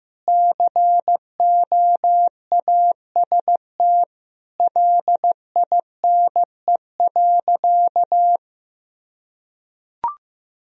Morse code, heard as a keyed tone, about 16 wpm, 700 Hz. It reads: COAST LINE.
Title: Sentences from Top 1000 Words - Review Part 4 - 15wpm